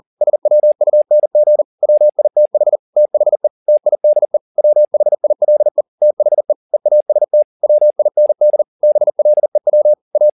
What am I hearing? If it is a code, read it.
Morse code, heard as a keyed tone, about 40 wpm, 600 Hz. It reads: SWUNG WITH THE TIDE WHILE THE EAST WIND BLEW A
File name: the old man and the sea 2 40wpm